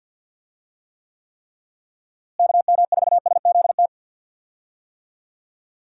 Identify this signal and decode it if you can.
Morse code, heard as a keyed tone, about 49 wpm, 700 Hz. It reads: KM4SZT
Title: USA Call Signs - 6 Characters - Repeated 3x - 50wpm